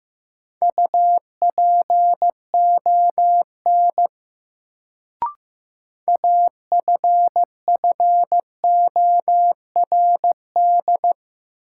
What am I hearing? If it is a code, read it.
Morse code, heard as a keyed tone, about 15 wpm, 700 Hz. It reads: UPON AFFORD
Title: Sentences from Top 1100 Words - New Word Drill Alt - 15wpm